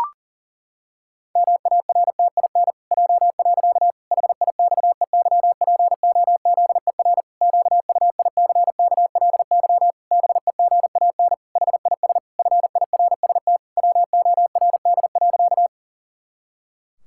Morse code, heard as a keyed tone, about 40 wpm, 700 Hz. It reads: MARTIN J. HIXEYPOZER QUICKLY BEGAN HIS FIRST WORD.
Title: Pangram Challenge v2 - 40wpm